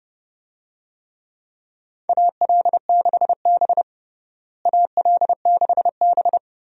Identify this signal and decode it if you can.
Morse code, heard as a keyed tone, about 32 wpm, 700 Hz. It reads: AL6B AL6B
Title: USA Call Signs - 4 Characters - Repeated 3x - 30wpm